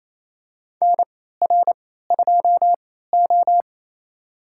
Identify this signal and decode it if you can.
Morse code, heard as a keyed tone, about 28 wpm, spaced as for 16 wpm, 700 Hz. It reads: NR2O
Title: USA Call Signs - 4 Characters - 3x Character Spacing - 28wpm